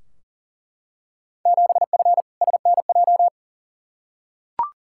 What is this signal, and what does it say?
Morse code, heard as a keyed tone, about 40 wpm, 700 Hz. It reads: 7F SNJ
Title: Field Day Exchanges Encore - 40wpm